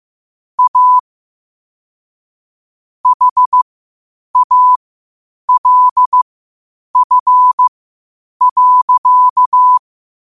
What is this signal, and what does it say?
Morse code, heard as a keyed tone, about 15 wpm, spaced as for 8 wpm, 1000 Hz. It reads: A HALF.
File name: qotdfiles / 8wpm